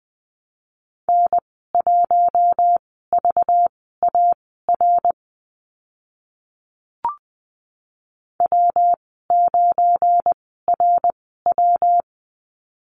Morse code, heard as a keyed tone, about 20 wpm, spaced as for 15 wpm, 700 Hz. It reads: N1VAR W9RW
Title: 193 Lesson 31g - Letter v - Call Signs 20wpm